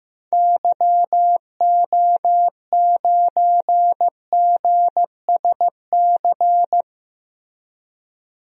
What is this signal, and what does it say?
Morse code, heard as a keyed tone, about 15 wpm, 700 Hz. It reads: YO9GSC
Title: DXCC-100 Call Signs - 6 Characters - Repeated 3x - 15wpm